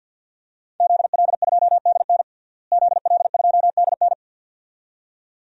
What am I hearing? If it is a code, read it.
Morse code, heard as a keyed tone, about 52 wpm, 700 Hz. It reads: ZL2DN ZL2DN
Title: DXCC-100 Call Signs - 5 Characters - Repeated 2x - 50wpm